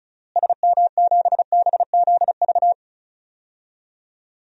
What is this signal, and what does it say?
Morse code, heard as a keyed tone, about 36 wpm, 700 Hz. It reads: SM7BZV
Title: DXCC-100 Call Signs - 6 Characters - Repeated 3x - 35wpm